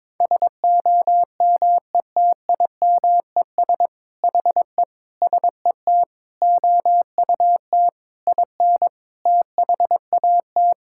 Morse code, heard as a keyed tone, about 22 wpm, 700 Hz. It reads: SOMETIMES HE SET OUT IN THAT
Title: Sentences from Top 1000 Words - Review Part 5 - 22wpm